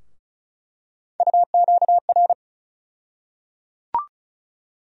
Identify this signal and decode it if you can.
Morse code, heard as a keyed tone, about 34 wpm, 700 Hz. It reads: UQR
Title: USA Call Sign Postfixes - 3 Letters 35wpm